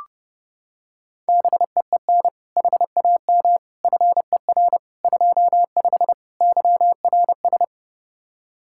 Morse code, heard as a keyed tone, about 31 wpm, 700 Hz. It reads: BEEN HAM FER 25 YRS